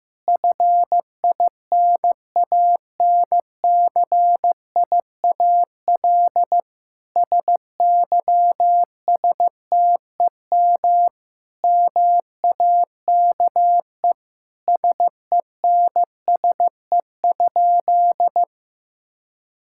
Morse code, heard as a keyed tone, about 15 wpm, 700 Hz. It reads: FINANCIAL SYSTEM MAKE SENSE?
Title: Sentences from Top 1000 Words - Review Part 5 - 15wpm